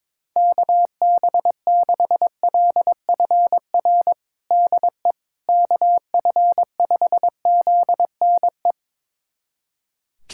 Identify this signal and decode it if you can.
Morse code, heard as a keyed tone, about 22 wpm, 700 Hz. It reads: KB6LFR DE KF5ZNE
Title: USA Callsign DE Callsign 22wpm